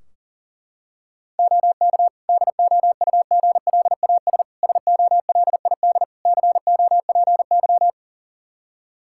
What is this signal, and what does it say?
Morse code, heard as a keyed tone, about 40 wpm, 700 Hz. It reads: OK DOUGLAS SOLID COPY
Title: Practice QSO - Exchange 3 - More Info - 40wpm